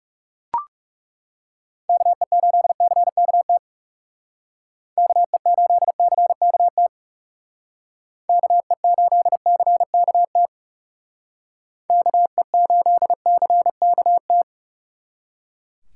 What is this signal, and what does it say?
Morse code, heard as a keyed tone, about 38 wpm, 700 Hz. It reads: KE8CKT KE8CKT KE8CKT KE8CKT